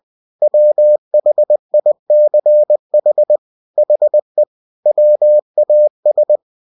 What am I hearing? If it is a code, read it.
Morse code, heard as a keyed tone, about 20 wpm, 600 Hz. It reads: WHICH HE WAS
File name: cw-books / winnie-the-pooh / winnie-20wpm-page-0092